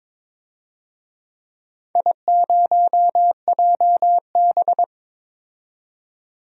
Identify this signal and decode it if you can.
Morse code, heard as a keyed tone, about 22 wpm, 700 Hz. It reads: I0JB
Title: DXCC-100 Call Signs - 4 Characters - 22wpm